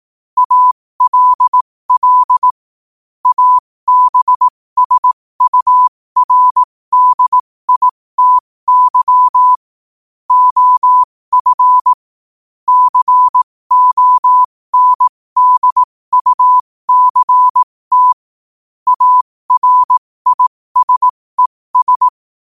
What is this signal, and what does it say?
Morse code, heard as a keyed tone, about 18 wpm, spaced as for 15 wpm, 1000 Hz. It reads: ALL ABSURDITY OF CONDUCT ARISES
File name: qotdfiles / 15wpm